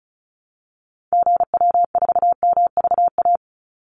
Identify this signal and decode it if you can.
Morse code, heard as a keyed tone, about 34 wpm, 700 Hz. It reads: GW4MVA
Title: Call Signs - 35wpm